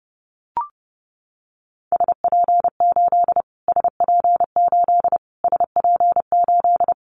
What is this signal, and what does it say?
Morse code, heard as a keyed tone, about 30 wpm, 700 Hz. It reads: SP8 SP8 SP8
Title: Callsign Prefixes for Top 25 DXCC - Repeated 3x - 30wpm